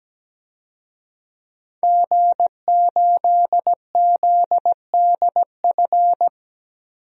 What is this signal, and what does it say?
Morse code, heard as a keyed tone, about 17 wpm, 700 Hz. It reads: G8ZDF
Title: DXCC-100 Call Signs - 5 Characters - 17wpm